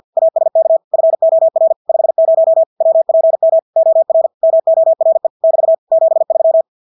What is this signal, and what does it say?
Morse code, heard as a keyed tone, about 50 wpm, 650 Hz. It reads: ASK FOR 50 WPM OR MORE <BT> 73
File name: Engelske ord 50 wpm